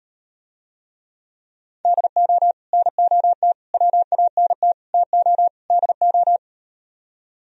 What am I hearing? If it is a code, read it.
Morse code, heard as a keyed tone, about 38 wpm, 700 Hz. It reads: DO NOT WANT TO DO